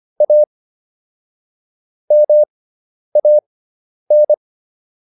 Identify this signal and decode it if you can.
Morse code, heard as a keyed tone, about 25 wpm, spaced as for 10 wpm, 600 Hz. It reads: A MAN